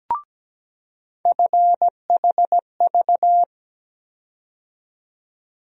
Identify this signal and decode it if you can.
Morse code, heard as a keyed tone, about 17 wpm, 700 Hz. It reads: FHV
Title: USA Call Sign Postfixes - 3 Letters 17wpm